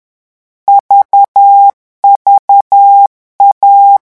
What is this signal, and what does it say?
Morse code, heard as a keyed tone, about 10 wpm, 800 Hz. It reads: VVA
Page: Leçon 3 - vitesse 10 mots minute